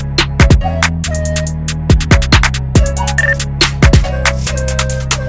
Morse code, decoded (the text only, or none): none